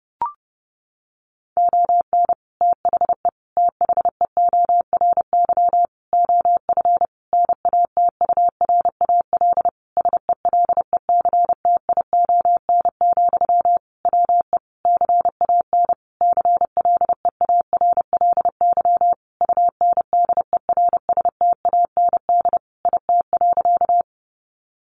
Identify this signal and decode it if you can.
Morse code, heard as a keyed tone, about 30 wpm, 700 Hz. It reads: ON THE THEORY OF NATURAL SELECTION, WE CAN CLEARLY UNDERSTAND IT.
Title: Sentences from Top 1400 Words 30wpm